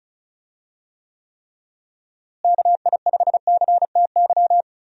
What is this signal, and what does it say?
Morse code, heard as a keyed tone, about 36 wpm, 700 Hz. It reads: KI5CTY
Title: USA Call Signs - 6 Characters - Repeated 3x - 35wpm